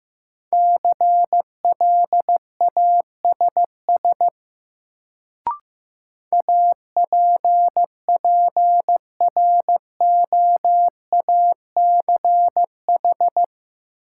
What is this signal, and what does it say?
Morse code, heard as a keyed tone, about 15 wpm, 700 Hz. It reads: CLASS APPROACH
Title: Sentences from Top 1400 Words - New Word Drill Alt - 15wpm